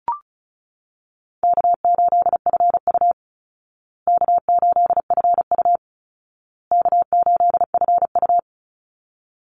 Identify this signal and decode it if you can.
Morse code, heard as a keyed tone, about 34 wpm, 700 Hz. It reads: K8FU K8FU K8FU
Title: Easy Call Signs Repeated 3x - 35wpm